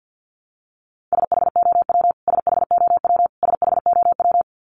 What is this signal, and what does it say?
Morse code, heard as a keyed tone, about 48 wpm, 700 Hz. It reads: SHOW SHOW SHOW
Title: Top 300 Words Repeated 50wpm